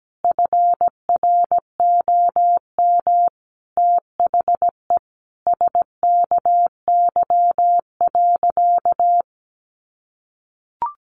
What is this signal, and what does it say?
Morse code, heard as a keyed tone, about 17 wpm, 700 Hz. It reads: FROM THE SKY.
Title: Sentences from Top 1200 Words 17wpm